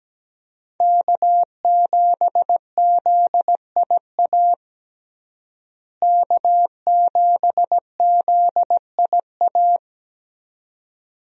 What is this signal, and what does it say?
Morse code, heard as a keyed tone, about 17 wpm, 700 Hz. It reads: K7ZIA K7ZIA
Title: USA Call Signs - 5 Characters - Repeated 2x - 17wpm